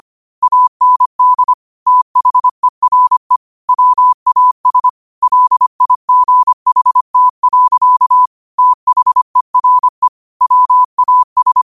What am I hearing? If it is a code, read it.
Morse code, heard as a keyed tone, about 25 wpm, 1000 Hz. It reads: AND THERE WAS LIGHT. THERE WAS